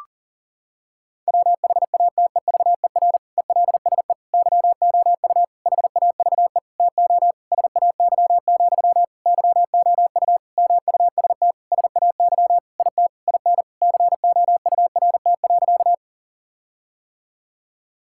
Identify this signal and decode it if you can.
Morse code, heard as a keyed tone, about 40 wpm, 700 Hz. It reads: WHATEVER ELSE YOU HAVE TO SAY, YOU MUST SAY IT IN COURT.
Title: Sentences from Top 1600 Words 40wpm